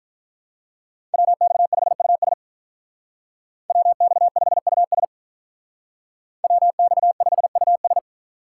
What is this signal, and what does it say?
Morse code, heard as a keyed tone, about 44 wpm, 700 Hz. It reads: WX5US WX5US WX5US
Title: Speed Racing - USA 2x2 Call Signs 35wpm